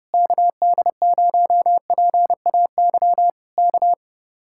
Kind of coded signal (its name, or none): Morse code